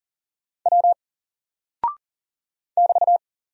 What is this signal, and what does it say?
Morse code, heard as a keyed tone, about 39 wpm, 700 Hz. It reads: W <BT>
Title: Single Character (Letter:Number:Punctuation:Prosign) 40wpm